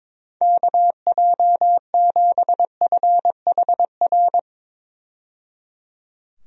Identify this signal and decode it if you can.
Morse code, heard as a keyed tone, about 22 wpm, 700 Hz. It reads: KJ7FHR
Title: USA Call Signs - 6 Characters - 22wpm